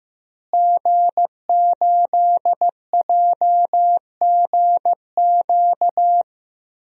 Morse code, heard as a keyed tone, about 15 wpm, 700 Hz. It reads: G8JGQ
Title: DXCC-100 Call Signs - 5 Characters - Repeated 2x - 15wpm